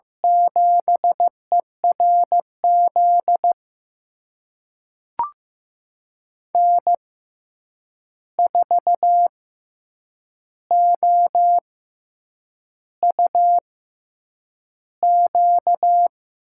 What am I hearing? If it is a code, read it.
Morse code, heard as a keyed tone, about 15 wpm, spaced as for 7 wpm, 700 Hz. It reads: 7ERZ N4OUQ
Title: USA Call Signs - 5 Characters - 6x Character Spacing - 15wpm